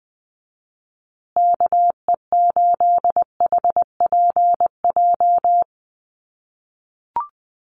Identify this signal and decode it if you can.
Morse code, heard as a keyed tone, about 20 wpm, 700 Hz. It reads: KE8HPJ
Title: USA Call Signs Repeated 3x 20wpm